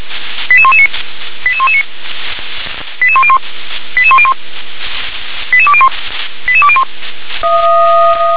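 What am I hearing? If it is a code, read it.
Five-tone selective call (ZVEI), heard as quick runs of five tones, tones of about 70 ms. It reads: 88119 88119 88101 88101 88201 88201
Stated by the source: Five-tone ZVEI selective call 88176, received on an SDR.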